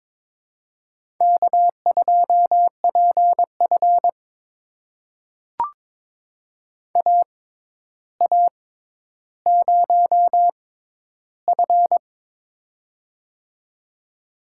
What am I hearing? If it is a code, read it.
Morse code, heard as a keyed tone, about 21 wpm, spaced as for 11 wpm, 700 Hz. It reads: K2PF AA0F
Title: USA Call Signs - 4 Characters - 6x Character Spacing - 22wpm